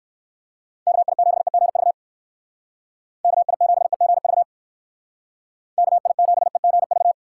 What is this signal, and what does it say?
Morse code, heard as a keyed tone, about 59 wpm, 700 Hz. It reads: KI7EGV KI7EGV KI7EGV